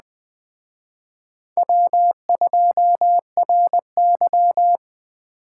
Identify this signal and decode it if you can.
Morse code, heard as a keyed tone, about 20 wpm, 700 Hz. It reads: W2RY
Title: Speed Racing - USA 1x2 Call Signs 17wpm